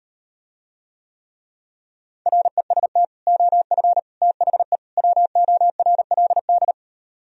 Morse code, heard as a keyed tone, about 38 wpm, 700 Hz. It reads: REST OF THE WORLD